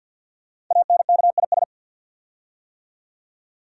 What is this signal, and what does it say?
Morse code, heard as a keyed tone, about 48 wpm, 700 Hz. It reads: ANKIS
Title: Code Groups - 5 Letters - 50wpm